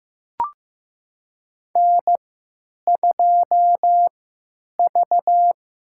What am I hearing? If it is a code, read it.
Morse code, heard as a keyed tone, about 15 wpm, spaced as for 9 wpm, 700 Hz. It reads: N2V
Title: USA Call Signs - 3x Character Spacing - 15wpm